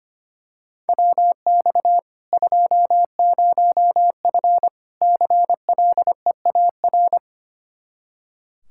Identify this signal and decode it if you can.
Morse code, heard as a keyed tone, about 25 wpm, 700 Hz. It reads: WX 20F CLEAR
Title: K1USN SST QSO Practice 25wpm